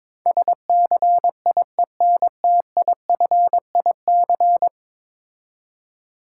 Morse code, heard as a keyed tone, about 23 wpm, 700 Hz. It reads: SCIENTIFIC
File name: practice / 22wpm / 10 Letter Words - First 3 Letters Given - 22wpm